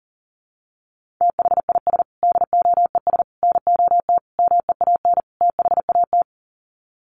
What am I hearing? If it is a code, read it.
Morse code, heard as a keyed tone, about 40 wpm, 700 Hz. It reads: THIS DOES NOT MEAN THAT
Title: Sets of 5 Words Encore2 40wpm